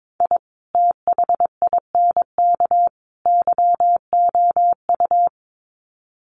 Morse code, heard as a keyed tone, about 22 wpm, 700 Hz. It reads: I THINK YOU
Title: Sets of 3 Words 22wpm